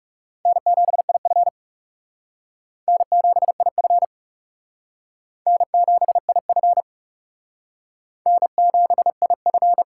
Morse code, heard as a keyed tone, about 37 wpm, 700 Hz. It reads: N7IF N7IF N7IF N7IF
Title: Speed Racing - USA 1x2 Call Signs 30wpm